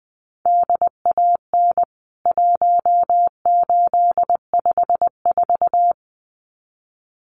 Morse code, heard as a keyed tone, about 20 wpm, 700 Hz. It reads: DAN 1854
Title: CWT Exchange 20wpm